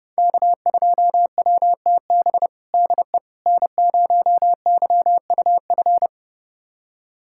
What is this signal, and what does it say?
Morse code, heard as a keyed tone, about 30 wpm, 700 Hz. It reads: K2WTB DE N0YUF
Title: USA Callsign DE Callsign - 5 Characters - 30wpm